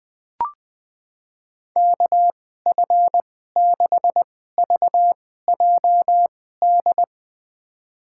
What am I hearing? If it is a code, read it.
Morse code, heard as a keyed tone, about 20 wpm, spaced as for 14 wpm, 700 Hz. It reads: KF6VJD
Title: USA Call Signs - 6 Characters - 2x Character Spacing - 20wpm